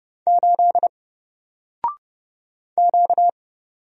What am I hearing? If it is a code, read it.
Morse code, heard as a keyed tone, about 30 wpm, 700 Hz. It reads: 8 Q
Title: Single Letter-Number Alt 30wpm